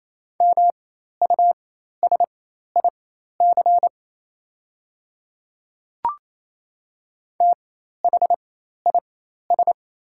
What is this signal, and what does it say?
Morse code, heard as a keyed tone, about 28 wpm, spaced as for 13 wpm, 690 Hz. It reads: MUSIC THIS
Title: 4x Character Spacing - Top 300 Words - 28wpm